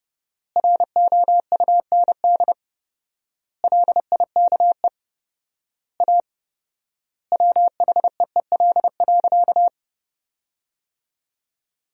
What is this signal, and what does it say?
Morse code, heard as a keyed tone, about 30 wpm, 700 Hz. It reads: ROUND LIKE A WHEEL.
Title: 4x Word-Spacing - Sentences from Top 500 Words - 30wpm